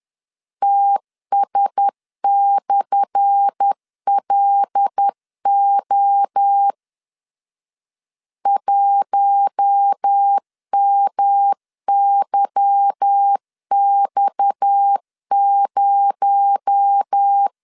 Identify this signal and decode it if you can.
Morse code, heard as a keyed tone, about 10 wpm, 800 Hz. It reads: TS/LO 1MYX0
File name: LEZIONI / 10 SIGNS / 10A SLASH 10wpm 2808